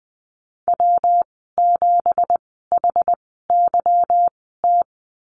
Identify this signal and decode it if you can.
Morse code, heard as a keyed tone, about 19 wpm, spaced as for 14 wpm, 700 Hz. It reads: W7HYT
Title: 167 Lesson 27g - Number 7 - Call Signs 20wpm